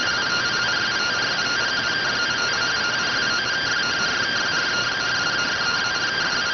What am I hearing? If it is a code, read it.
Two-tone teleprinter signal, about 170 Hz shift.